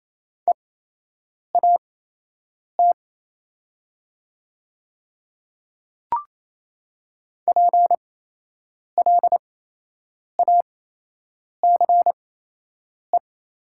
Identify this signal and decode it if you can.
Morse code, heard as a keyed tone, about 27 wpm, spaced as for 8 wpm, 690 Hz. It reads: EAT PLACE
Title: 8x Character Spacing - Top 500 Words - 28wpm